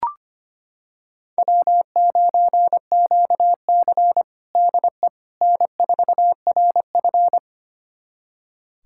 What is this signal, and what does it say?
Morse code, heard as a keyed tone, about 25 wpm, 700 Hz. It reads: W9QC DE N4RF
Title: USA Callsign DE Callsign - 4 Characters - 25wpm